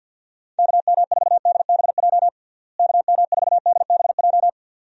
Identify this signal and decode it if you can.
Morse code, heard as a keyed tone, about 51 wpm, 700 Hz. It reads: KM4DBJ KM4DBJ
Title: USA Call Signs - 6 Characters - Repeated 2x - 50wpm